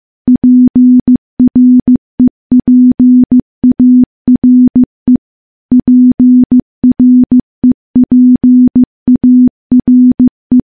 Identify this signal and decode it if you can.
Morse code, heard as a keyed tone, about 15 wpm, 250 Hz. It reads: PREPARE PREPARE
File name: morse / morse-training-15WPM